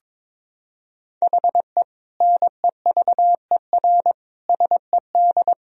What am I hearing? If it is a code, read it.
Morse code, heard as a keyed tone, about 22 wpm, 700 Hz. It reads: HE NEVER SED
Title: Sentences from Top 500 Words and Abbreviations 22wpm